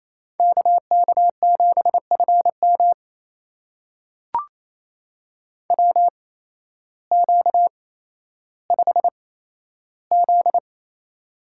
Morse code, heard as a keyed tone, about 28 wpm, spaced as for 13 wpm, 700 Hz. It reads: KK7FM WQ5Z
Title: USA Call Signs - 8x Character Spacing - 28wpm